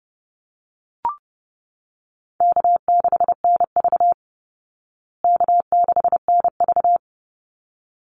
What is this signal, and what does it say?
Morse code, heard as a keyed tone, about 30 wpm, 700 Hz. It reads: K6NV K6NV
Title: Easy Call Signs Repeated 2x - 30wpm